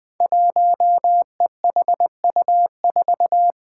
Morse code, heard as a keyed tone, about 20 wpm, 700 Hz. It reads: 1EHU4
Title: Five Challenging Letter-Number Combinations 20wpm